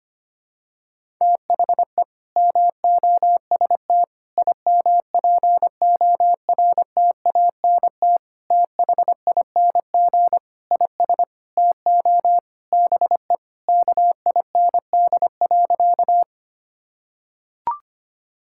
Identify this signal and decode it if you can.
Morse code, heard as a keyed tone, about 25 wpm, 700 Hz. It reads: THE MOST IMPORTANT THING IS TO BE KIND.
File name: practice / 25wpm / Sentences with 4-Word Phrases - 25wpm